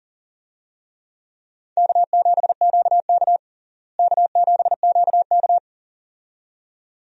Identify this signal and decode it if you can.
Morse code, heard as a keyed tone, about 40 wpm, 700 Hz. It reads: K7QK K7QK
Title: USA Call Signs - 4 Characters - Repeated 2x - 40wpm